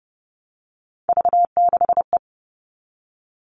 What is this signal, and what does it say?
Morse code, heard as a keyed tone, about 30 wpm, 700 Hz. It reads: V6E
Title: All Cut Numbers - 3-Digit 30wpm